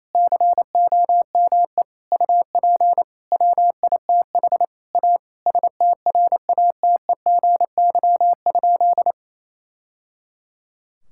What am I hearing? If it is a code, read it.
Morse code, heard as a keyed tone, about 28 wpm, 700 Hz. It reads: COME UP WITH A STRATEGY?
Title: Sentences from Top 1000 Words - Review Part 5 - 28wpm